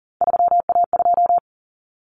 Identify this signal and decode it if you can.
Morse code, heard as a keyed tone, about 40 wpm, 700 Hz. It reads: 3A2
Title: Common Cut Numbers - 3-Digit 40wpm